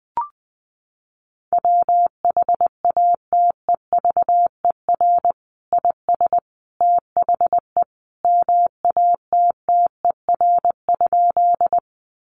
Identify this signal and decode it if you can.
Morse code, heard as a keyed tone, about 20 wpm, 700 Hz. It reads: WHATEVER IS THE MATTER?
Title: Sentences from Top 1100 Words 20wpm